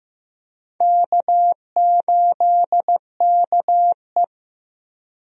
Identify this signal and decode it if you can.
Morse code, heard as a keyed tone, about 15 wpm, 700 Hz. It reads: K8KE
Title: USA Call Signs - 4 Characters - Repeated 3x - 15wpm